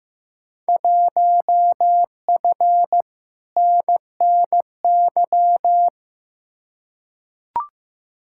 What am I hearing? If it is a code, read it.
Morse code, heard as a keyed tone, about 15 wpm, 700 Hz. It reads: 1F NNY
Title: Field Day Exchanges - 15wpm